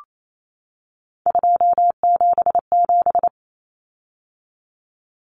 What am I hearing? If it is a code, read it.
Morse code, heard as a keyed tone, about 28 wpm, 700 Hz. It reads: 277
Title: Common Cut Numbers - 3-Digit 28wpm